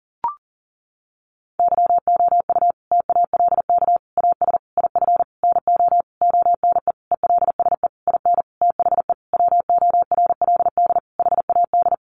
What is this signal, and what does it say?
Morse code, heard as a keyed tone, about 40 wpm, 700 Hz. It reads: YOU TALK AS IF NO ONE ELSE IN THE WORLD HAD
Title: Sentences from Top 1100 Words 40wpm